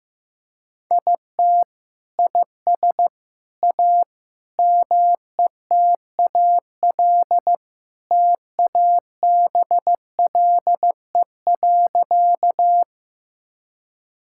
Morse code, heard as a keyed tone, about 15 wpm, 700 Hz. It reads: IT IS A METAL TABLE.
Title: Sentences from Top 1000 Words - Review Part 6 - 15wpm